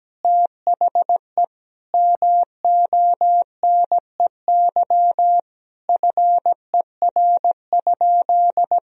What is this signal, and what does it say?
Morse code, heard as a keyed tone, about 17 wpm, 700 Hz. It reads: THE MONEY FER?
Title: Sentences from Top 400 Words and Abbreviations 17wpm